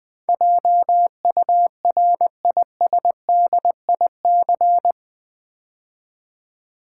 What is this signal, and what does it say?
Morse code, heard as a keyed tone, about 20 wpm, 700 Hz. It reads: JURISDIC
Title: Build A Word - 4 Letters At A Time - 12-Letter Words - 20wpm